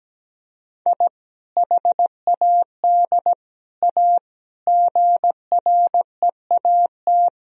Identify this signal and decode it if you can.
Morse code, heard as a keyed tone, about 17 wpm, 700 Hz. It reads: I HAD A GREAT
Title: Sentences from Top 1000 Words - Review Part 2 - 17wpm